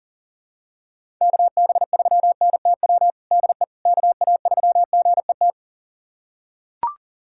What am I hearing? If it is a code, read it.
Morse code, heard as a keyed tone, about 39 wpm, 700 Hz. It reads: KB3NTW DE KA3GET
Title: USA Callsign DE Callsign - 6 Characters - 40wpm